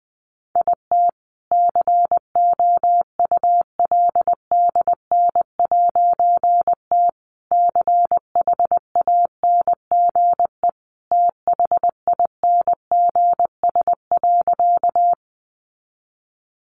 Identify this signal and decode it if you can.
Morse code, heard as a keyed tone, about 19 wpm, 700 Hz. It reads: IT COULDN'T CHANGE THINGS.
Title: A Sound Of Thunder 20wpm